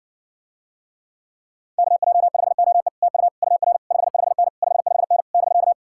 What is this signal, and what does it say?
Morse code, heard as a keyed tone, about 60 wpm, 700 Hz. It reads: KJ5OE TU UR 55N 55N <BK>